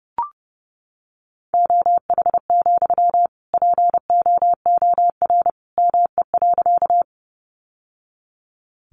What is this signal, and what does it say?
Morse code, heard as a keyed tone, about 30 wpm, 700 Hz. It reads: OH, POOR ME.
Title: Sentences from Top 1000 Words 30wpm